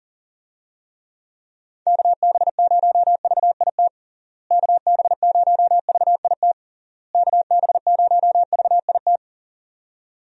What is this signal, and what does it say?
Morse code, heard as a keyed tone, about 40 wpm, 700 Hz. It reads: KB0VIT KB0VIT KB0VIT
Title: USA Call Signs - 6 Characters - Repeated 3x - 40wpm